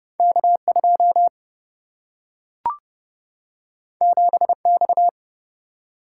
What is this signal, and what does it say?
Morse code, heard as a keyed tone, about 30 wpm, 700 Hz. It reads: K 2 7 X
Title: Single-Letter Single-Number 30wpm